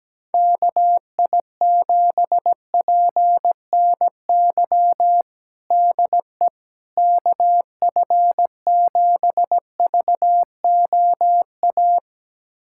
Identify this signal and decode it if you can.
Morse code, heard as a keyed tone, about 17 wpm, 700 Hz. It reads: KI7PNY DE KF7VOA
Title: USA Callsign DE Callsign - 6 Characters - 17wpm